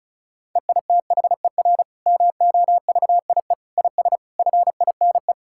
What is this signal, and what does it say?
Morse code, heard as a keyed tone, about 36 wpm, 700 Hz. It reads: EITHER MOVIE IS FINE
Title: Sentences from Top 1000 Words - Review Part 5 - 35wpm